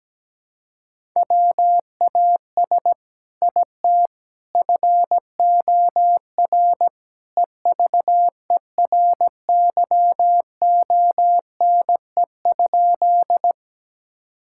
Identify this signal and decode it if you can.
Morse code, heard as a keyed tone, about 17 wpm, 700 Hz. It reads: WAS IT FOR EVERYONE?
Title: Sentences from Top 1000 Words - Review Part 5 - 17wpm